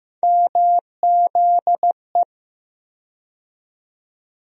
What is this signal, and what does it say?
Morse code, heard as a keyed tone, about 15 wpm, 700 Hz. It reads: MZE
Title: USA Call Sign Postfixes - 3 Letters 15wpm